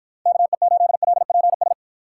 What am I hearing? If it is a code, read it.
Morse code, heard as a keyed tone, about 53 wpm, 700 Hz. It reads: KE8LPS